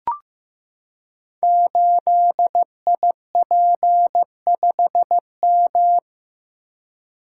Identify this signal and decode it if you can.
Morse code, heard as a keyed tone, about 15 wpm, 700 Hz. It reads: 8IP5M
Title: Code Groups - 5 Letter:Numbers - 15wpm